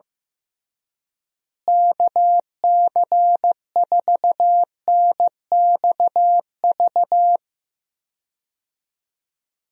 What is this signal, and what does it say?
Morse code, heard as a keyed tone, about 15 wpm, 700 Hz. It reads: KC4NXV
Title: USA Call Signs - 6 Characters - Repeated 3x - 15wpm